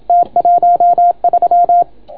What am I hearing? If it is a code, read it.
Morse code, heard as a keyed tone, about 28 wpm, 690 Hz. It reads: T13